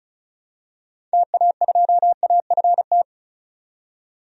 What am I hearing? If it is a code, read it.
Morse code, heard as a keyed tone, about 36 wpm, 700 Hz. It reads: TA2AFT
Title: DXCC-100 Call Signs - 6 Characters - 35wpm